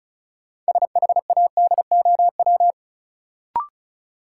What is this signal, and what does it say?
Morse code, heard as a keyed tone, about 36 wpm, 700 Hz. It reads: SHADOW
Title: Sets of 5 Words Encore - New Word Drill - 35wpm